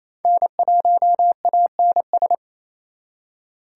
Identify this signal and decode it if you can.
Morse code, heard as a keyed tone, about 36 wpm, 700 Hz. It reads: N1ANS